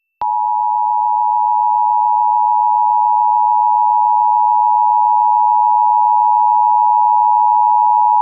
Emergency Alert System (SAME) tones (the 853 and 960 Hz attention tone). It is heard from 1.1 to 7.1 s.